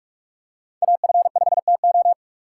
Morse code, heard as a keyed tone, about 41 wpm, 700 Hz. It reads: AF5TO